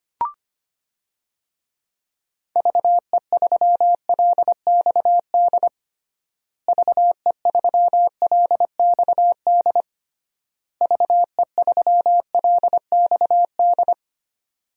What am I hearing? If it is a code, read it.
Morse code, heard as a keyed tone, about 25 wpm, 700 Hz. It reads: VE3LXD VE3LXD VE3LXD
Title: DXCC-100 Call Signs - 6 Characters - Repeated 3x - 25wpm